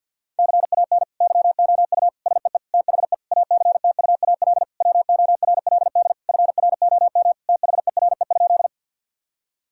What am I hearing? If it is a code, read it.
Morse code, heard as a keyed tone, about 50 wpm, 700 Hz. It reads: CAN YOU SEE THE ACTUAL WORLD FROM THERE?
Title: Sentences from Top 1000 Words - Review Part 5 - 50wpm